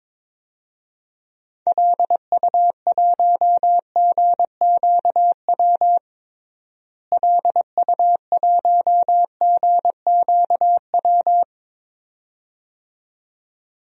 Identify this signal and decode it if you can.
Morse code, heard as a keyed tone, about 22 wpm, 700 Hz. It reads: LU1GQW LU1GQW
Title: DXCC-100 Call Signs - 6 Characters - Repeated 2x - 22wpm